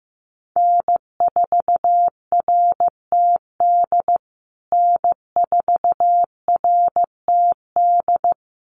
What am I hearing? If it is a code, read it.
Morse code, heard as a keyed tone, about 15 wpm, 700 Hz. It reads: N4RTD N4RTD
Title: USA Call Signs Repeated 3x 15wpm